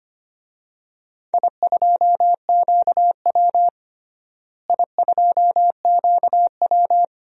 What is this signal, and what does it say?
Morse code, heard as a keyed tone, about 25 wpm, 700 Hz. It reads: I2QW I2QW
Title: DXCC-100 Call Signs - 4 Characters - Repeated 3x - 25wpm